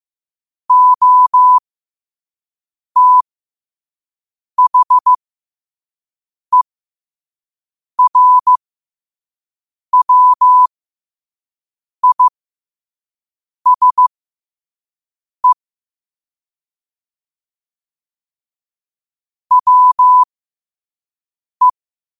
Morse code, heard as a keyed tone, about 15 wpm, spaced as for 5 wpm, 1000 Hz. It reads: OTHERWISE WE
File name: qotdfiles / 5wpm